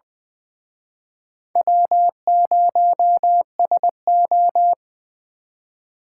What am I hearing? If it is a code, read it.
Morse code, heard as a keyed tone, about 25 wpm, 700 Hz. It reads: W0SO
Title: Speed Racing - USA 1x2 Call Signs 20wpm